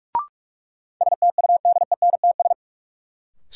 Morse code, heard as a keyed tone, about 46 wpm, 700 Hz. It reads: STUDENTS
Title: Sets of 5 Words Encore - New Word Drill - 45wpm